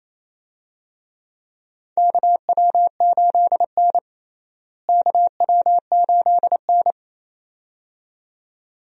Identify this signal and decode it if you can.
Morse code, heard as a keyed tone, about 27 wpm, 700 Hz. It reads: KW8N KW8N
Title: USA Call Signs - 4 Characters - Repeated 2x - 28wpm